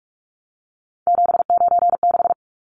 Morse code, heard as a keyed tone, about 44 wpm, 700 Hz. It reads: 796
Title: Numbers - 3-Digit 45wpm